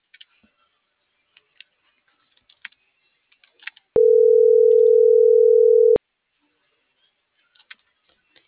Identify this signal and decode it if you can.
Ringback tone (North America), 440 + 480 Hz.